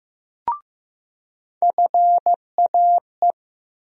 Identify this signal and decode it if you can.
Morse code, heard as a keyed tone, about 15 wpm, 700 Hz. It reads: FAE
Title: USA Call Sign Postfixes - 3 Letters 15wpm